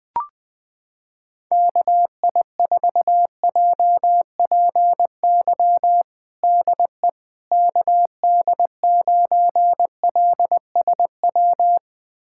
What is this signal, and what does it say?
Morse code, heard as a keyed tone, about 20 wpm, 700 Hz. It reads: KI4JPY DE KD9LSW
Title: USA Callsign DE Callsign 20wpm